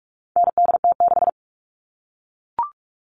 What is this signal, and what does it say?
Morse code, heard as a keyed tone, about 44 wpm, 700 Hz. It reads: NDT6
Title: All Cut Numbers - 4-Digit 45wpm